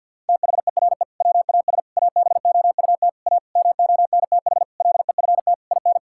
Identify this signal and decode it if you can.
Morse code, heard as a keyed tone, about 51 wpm, 700 Hz. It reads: THERE WAS ABOUT A MONTH LEFT IN
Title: Sentences from Top 1000 Words - Review Part 5 - 50wpm